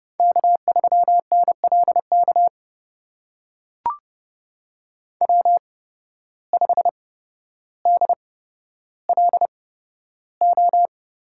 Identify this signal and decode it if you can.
Morse code, heard as a keyed tone, about 32 wpm, spaced as for 13 wpm, 700 Hz. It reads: K3NLK W5DLO
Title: USA Call Signs - 5 Characters - 8x Character Spacing - 30wpm